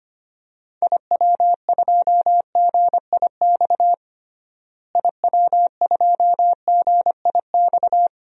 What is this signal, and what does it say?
Morse code, heard as a keyed tone, about 25 wpm, 700 Hz. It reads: IW2GIX IW2GIX
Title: DXCC-100 Call Signs - 6 Characters - Repeated 2x - 25wpm